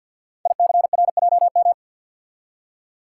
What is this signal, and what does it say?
Morse code, heard as a keyed tone, about 50 wpm, 700 Hz. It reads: ICRJM
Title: Code Groups - 5 Letters - 50wpm